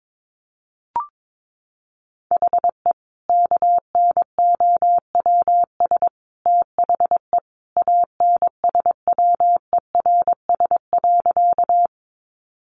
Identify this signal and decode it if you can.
Morse code, heard as a keyed tone, about 22 wpm, 700 Hz. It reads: HE KNOWS THE ANSWERS.
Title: Additional Easy Phrases 22wpm